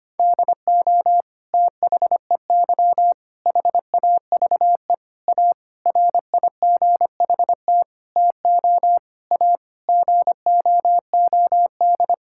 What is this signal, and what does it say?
Morse code, heard as a keyed tone, about 25 wpm, 700 Hz. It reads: DO THEY HAVE A RIGHT TO A GOOD
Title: Sentences from Top 1000 Words - Review Part 6 - 25wpm